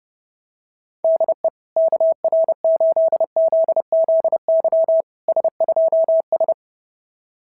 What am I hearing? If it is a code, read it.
Morse code, heard as a keyed tone, about 30 wpm, 690 Hz. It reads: DE KR8ZZY S2S
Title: SOTA QSO Practice 30wpm